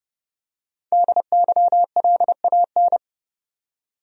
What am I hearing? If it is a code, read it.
Morse code, heard as a keyed tone, about 30 wpm, 700 Hz. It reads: DYLAN
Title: USA Common Names - 30wpm